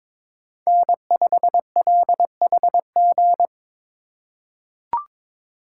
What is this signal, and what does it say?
Morse code, heard as a keyed tone, about 23 wpm, 700 Hz. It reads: N5LHG
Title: USA Call Signs - 5 Characters - 3x Character Spacing - 22wpm